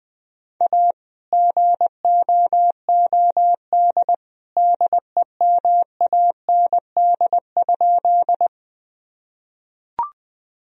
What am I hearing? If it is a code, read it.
Morse code, heard as a keyed tone, about 20 wpm, 700 Hz. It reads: A GOOD DEMAND?
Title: Sentences from Top 1000 Words - Review Part 4 - 20wpm